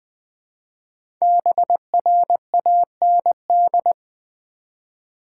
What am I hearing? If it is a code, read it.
Morse code, heard as a keyed tone, about 20 wpm, 700 Hz. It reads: BRAND
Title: 5 Letter Words - First 3 Letters Given - 20wpm